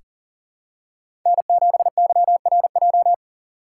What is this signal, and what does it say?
Morse code, heard as a keyed tone, about 38 wpm, 700 Hz. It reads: N7YRJ
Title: USA Call Signs - 5 Characters - 2x Character Spacing - 40wpm